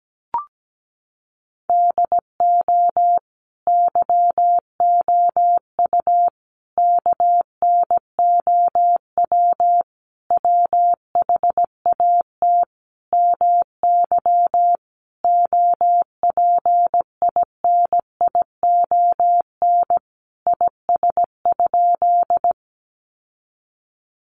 Morse code, heard as a keyed tone, about 17 wpm, 700 Hz. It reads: DO YOU KNOW WHAT MY OPINION IS?
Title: Sentences from Top 1200 Words 17wpm